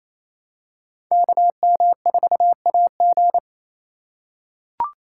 Morse code, heard as a keyed tone, about 28 wpm, 700 Hz. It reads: KM4AG
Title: USA Call Signs - 5 Characters - 4x Character Spacing - 28wpm